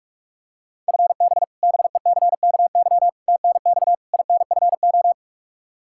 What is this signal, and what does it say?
Morse code, heard as a keyed tone, about 46 wpm, 700 Hz. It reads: FB BECKY TNX INFO
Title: Practice QSO - Exchange 3 - More Info - 45wpm